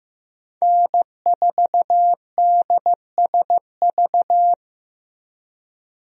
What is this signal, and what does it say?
Morse code, heard as a keyed tone, about 14 wpm, 700 Hz. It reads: N4DSV
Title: USA Call Signs - 5 Characters - Repeated 2x - 15wpm